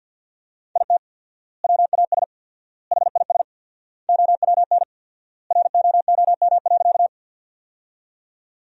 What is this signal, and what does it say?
Morse code, heard as a keyed tone, about 50 wpm, 700 Hz. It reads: IT WAS HIS OWN ROOM.
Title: 4x Word-Spacing - Sentences from Top 300 Words - 50wpm